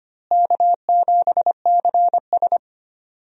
Morse code, heard as a keyed tone, about 25 wpm, 700 Hz. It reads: K7CS
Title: USA Call Signs - 4 Characters - Repeated 2x - 25wpm